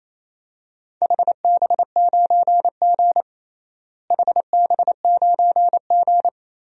Morse code, heard as a keyed tone, about 28 wpm, 700 Hz. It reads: HB9G HB9G
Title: DXCC-100 Call Signs - 4 Characters - Repeated 3x - 28wpm